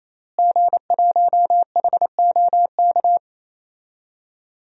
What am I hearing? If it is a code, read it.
Morse code, heard as a keyed tone, about 28 wpm, 700 Hz. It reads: G1HOK
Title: DXCC-100 Call Signs - 5 Characters - Repeated 3x - 28wpm